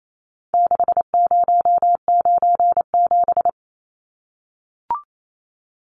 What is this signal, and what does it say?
Morse code, heard as a keyed tone, about 28 wpm, 700 Hz. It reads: 6097
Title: Numbers - 4-Digit 28wpm